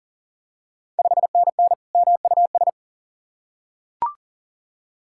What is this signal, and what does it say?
Morse code, heard as a keyed tone, about 40 wpm, 690 Hz. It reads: 5NN MUS
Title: Ontario QSO Party Exchange 40wpm